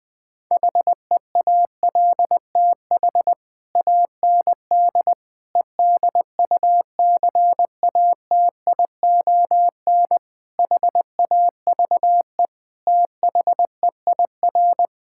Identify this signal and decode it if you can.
Morse code, heard as a keyed tone, about 20 wpm, 700 Hz. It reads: HEALTH AND EDUCATION HAVE THEIR
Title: Sentences from Top 1000 Words - Review Part 6 - 20wpm